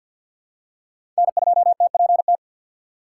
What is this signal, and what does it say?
Morse code, heard as a keyed tone, about 53 wpm, 700 Hz. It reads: N2TPT
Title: USA Call Signs - 5 Characters - 8x Character Spacing - 50wpm